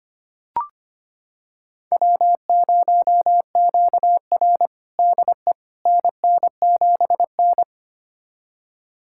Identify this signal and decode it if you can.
Morse code, heard as a keyed tone, about 25 wpm, 700 Hz. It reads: W0QR DE NN7N
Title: USA Callsign DE Callsign - 4 Characters - 25wpm